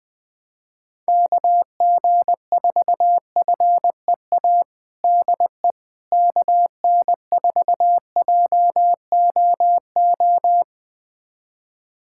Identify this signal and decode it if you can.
Morse code, heard as a keyed tone, about 20 wpm, 700 Hz. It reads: KG4FEA DE KN4JOO
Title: USA Callsign DE Callsign - 6 Characters - 20wpm